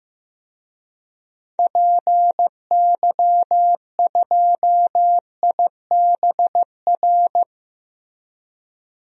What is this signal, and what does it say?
Morse code, heard as a keyed tone, about 15 wpm, 700 Hz. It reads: PY2IBR
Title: DXCC-100 Call Signs - 6 Characters - Repeated 2x - 15wpm